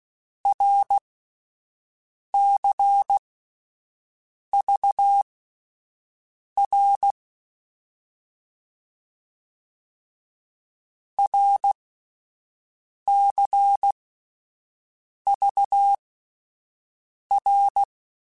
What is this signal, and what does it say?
Morse code, heard as a keyed tone, about 16 wpm, spaced as for 5 wpm, 800 Hz. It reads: RCVR RCVR
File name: Abreviations / Abreviations mp3s / Abreviations@5